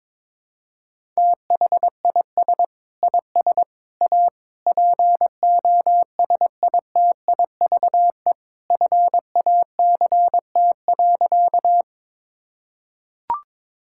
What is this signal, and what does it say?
Morse code, heard as a keyed tone, about 22 wpm, 700 Hz. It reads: THIS IS A POSITIVE FACT.
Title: Sentences from Top 1000 Words - Review Part 5 - 22wpm